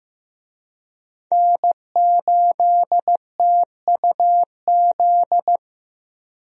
Morse code, heard as a keyed tone, about 15 wpm, 700 Hz. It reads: N8TUZ
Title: USA Call Signs - 5 Characters - 15wpm